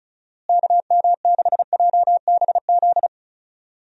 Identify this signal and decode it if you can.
Morse code, heard as a keyed tone, about 34 wpm, 700 Hz. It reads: KM6JBZ
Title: USA Call Signs - 6 Characters - Repeated 2x - 35wpm